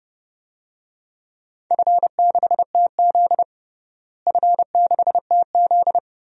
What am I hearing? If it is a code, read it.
Morse code, heard as a keyed tone, about 30 wpm, 700 Hz. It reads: F6TZ F6TZ
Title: DXCC-100 Call Signs - 4 Characters - Repeated 2x - 30wpm